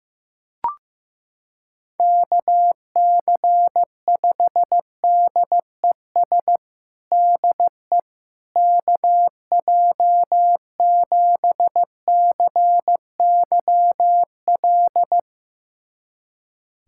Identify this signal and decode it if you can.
Morse code, heard as a keyed tone, about 16 wpm, 700 Hz. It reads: KC5DES DE KJ7CYL
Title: USA Callsign DE Callsign - 6 Characters - 15wpm